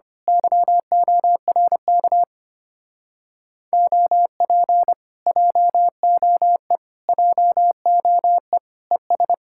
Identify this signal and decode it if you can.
Morse code, heard as a keyed tone, about 24 wpm, 700 Hz. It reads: YORK OP JOE JOE ES
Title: Speed-Racing Common QSO Phrases - 25wpm